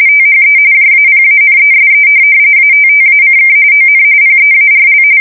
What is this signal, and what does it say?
RTTY, about 45.45 baud. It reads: THE JAPANESE ATTACK ON PEARL
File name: fsk170 / USS-New-Jersey